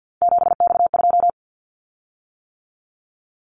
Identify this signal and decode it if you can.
Morse code, heard as a keyed tone, about 50 wpm, 700 Hz. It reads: 7X2
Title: Callsign Prefixes for Top 100 DXCC - 50wpm